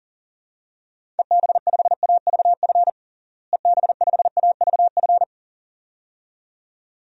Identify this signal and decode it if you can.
Morse code, heard as a keyed tone, about 40 wpm, 700 Hz. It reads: EB5AVF EB5AVF
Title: DXCC-100 Call Signs - 6 Characters - Repeated 2x - 40wpm